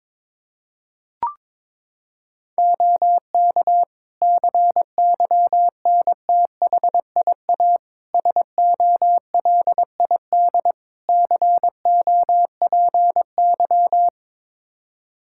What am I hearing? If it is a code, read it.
Morse code, heard as a keyed tone, about 22 wpm, 700 Hz. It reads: OK CYNTHIA SOLID COPY
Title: K1USN SST QSO Practice 22wpm